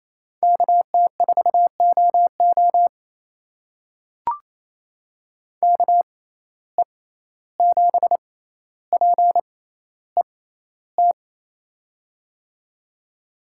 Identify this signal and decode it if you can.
Morse code, heard as a keyed tone, about 28 wpm, spaced as for 14 wpm, 700 Hz. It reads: KT4OO KE7PET
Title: USA Call Signs - 6x Character Spacing - 28wpm